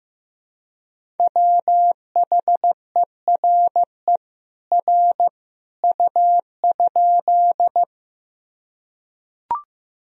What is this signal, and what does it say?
Morse code, heard as a keyed tone, about 15 wpm, 700 Hz. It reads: WHERE R U?
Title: Sentences from Top 200 Words and Abbreviations 15wpm